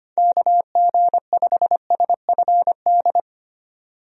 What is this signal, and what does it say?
Morse code, heard as a keyed tone, about 25 wpm, 700 Hz. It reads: KG5SFD
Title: USA Call Signs - 6 Characters - 25wpm